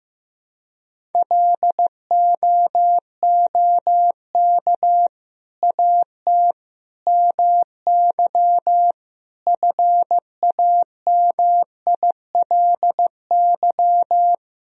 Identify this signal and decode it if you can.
Morse code, heard as a keyed tone, about 15 wpm, 700 Hz. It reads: LOOK AT MY FAMILY
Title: Sentences from Top 1000 Words - Review Part 1 - 15wpm